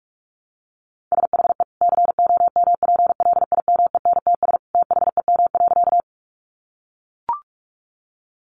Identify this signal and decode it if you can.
Morse code, heard as a keyed tone, about 44 wpm, 700 Hz. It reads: SHE COMPLIMENTS THEM.
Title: Easy Phrases 45wpm